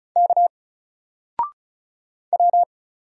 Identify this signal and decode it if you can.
Morse code, heard as a keyed tone, about 36 wpm, 700 Hz. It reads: K W
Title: Single Character (Letter:Number:Punctuation:Prosign) 35wpm